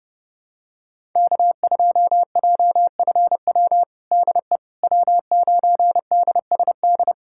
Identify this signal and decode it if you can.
Morse code, heard as a keyed tone, about 30 wpm, 700 Hz. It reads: K2JFW DE W9DSD
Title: USA Callsign DE Callsign - 5 Characters - 30wpm